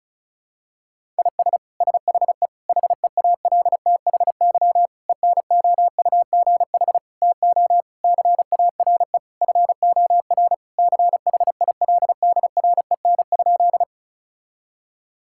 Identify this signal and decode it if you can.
Morse code, heard as a keyed tone, about 34 wpm, 700 Hz. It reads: IS SHE HEALTHY ENOUGH TO CARE FOR CHILDREN?
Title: Sentences from Top 1000 Words - Review Part 5 - 35wpm